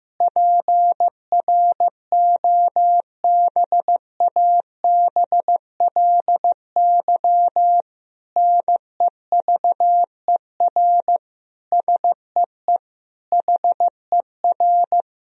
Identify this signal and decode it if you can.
Morse code, heard as a keyed tone, about 15 wpm, 700 Hz. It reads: PROBABLY NEVER SEE HER
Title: Sentences from Top 1000 Words - Review Part 4 - 15wpm